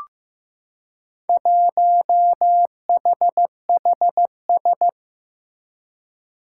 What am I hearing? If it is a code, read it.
Morse code, heard as a keyed tone, about 15 wpm, 700 Hz. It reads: 1HHS
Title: Four Challenging Letter-Number Combinations 15wpm